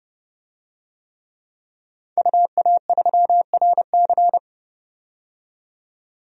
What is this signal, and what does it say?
Morse code, heard as a keyed tone, about 30 wpm, 700 Hz. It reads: UA3RC
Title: DXCC-100 Call Signs - 5 Characters - 30wpm